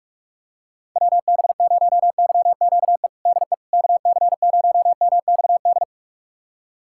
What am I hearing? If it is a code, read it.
Morse code, heard as a keyed tone, about 46 wpm, 700 Hz. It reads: WB0YQE DE KC0MXD
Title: USA Callsign DE Callsign - 6 Characters - 45wpm